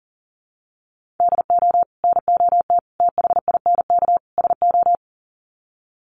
Morse code, heard as a keyed tone, about 40 wpm, 700 Hz. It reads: DO NOT THINK SO
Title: Sets of 4 Words 40wpm